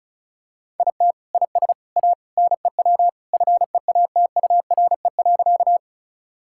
Morse code, heard as a keyed tone, about 34 wpm, 700 Hz. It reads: IT IS A NEW FEATURE.
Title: Sentences from Top 1000 Words - Review Part 5 - 35wpm